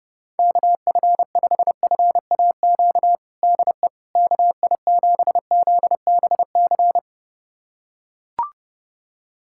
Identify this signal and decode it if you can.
Morse code, heard as a keyed tone, about 30 wpm, 700 Hz. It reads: KF5FAQ DE KI7ZBC
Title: USA Callsign DE Callsign 30wpm